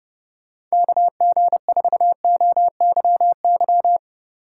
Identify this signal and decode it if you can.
Morse code, heard as a keyed tone, about 31 wpm, 700 Hz. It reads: KG4OYY